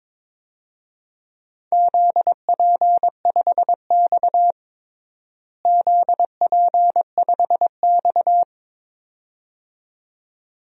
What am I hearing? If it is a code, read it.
Morse code, heard as a keyed tone, about 22 wpm, 700 Hz. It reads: ZP5X ZP5X
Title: DXCC-100 Call Signs - 4 Characters - Repeated 2x - 22wpm